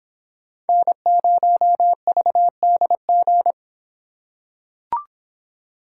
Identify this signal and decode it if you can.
Morse code, heard as a keyed tone, about 26 wpm, 700 Hz. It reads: N0VDG